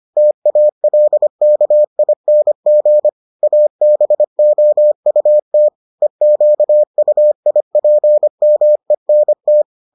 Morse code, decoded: TALKING ABOUT EQUIPMENT

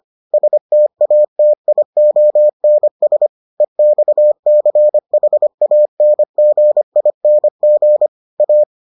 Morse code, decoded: STATIONS EXCHANGING A

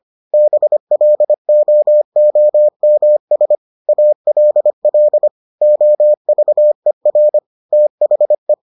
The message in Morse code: BLOOMS ALL OVER THE

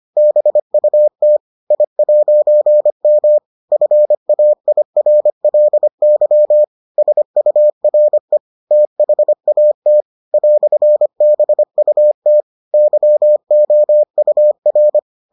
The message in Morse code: BUT I'M FAIRLY SURE THAT "BUT YOUR

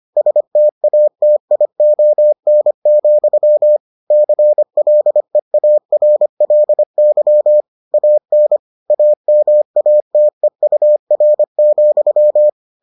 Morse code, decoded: STATION, CLEARLY AN AMATEUR,

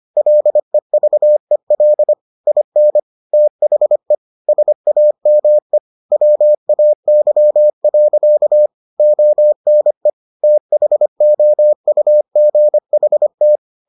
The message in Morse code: LEVEL IN THE SAME WAY. ONE THOUGHT

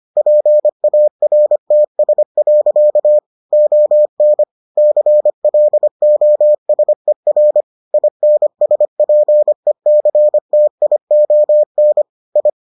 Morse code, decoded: PARTS. ON CLOSER INSPECTION I

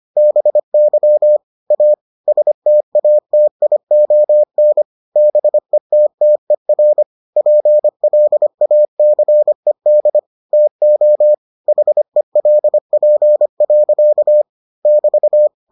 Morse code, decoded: BY A STATION BETTER PLACED TO HELP. <BT>

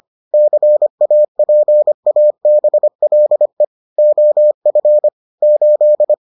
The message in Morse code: CAPABLE OF 8

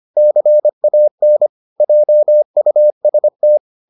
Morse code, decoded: CAN JUST

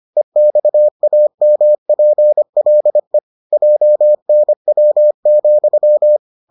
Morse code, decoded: EXAMPLE JNW,

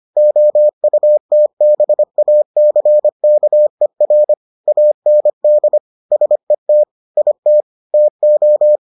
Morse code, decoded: OUTBACKER AND SET IT TO

